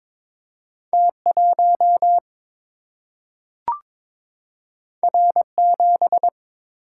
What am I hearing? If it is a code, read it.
Morse code, heard as a keyed tone, about 22 wpm, 700 Hz. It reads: T1 R7
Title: Single-Letter Single-Number 22wpm